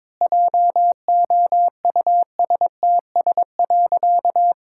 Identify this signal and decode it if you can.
Morse code, heard as a keyed tone, about 22 wpm, 700 Hz. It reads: JOUSTS.
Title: Pangram Challenge v2 - 22wpm